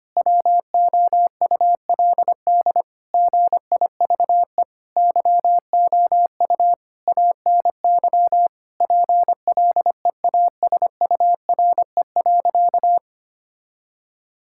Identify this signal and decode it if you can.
Morse code, heard as a keyed tone, about 25 wpm, 700 Hz. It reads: WOULD GIVE YOU ANY PLEASURE.
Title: Sentences from Top 1600 Words 25wpm